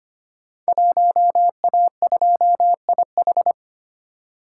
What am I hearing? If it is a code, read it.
Morse code, heard as a keyed tone, about 25 wpm, 700 Hz. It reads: 1A2IH
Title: Code Groups - 5 Letter:Numbers - 25wpm